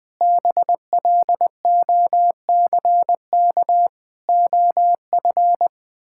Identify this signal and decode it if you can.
Morse code, heard as a keyed tone, about 20 wpm, 700 Hz. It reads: BLOCK OF
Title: Sentences from Top 1600 Words 20wpm